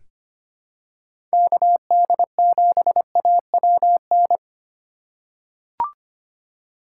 Morse code, heard as a keyed tone, about 25 wpm, 700 Hz. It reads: KD7AWN
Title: USA Call Signs - 4x Character Spacing - 25wpm